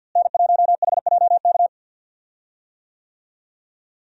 Morse code, heard as a keyed tone, about 51 wpm, 700 Hz. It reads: N1HJK
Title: USA Call Signs - 5 Characters - Repeated 2x - 50wpm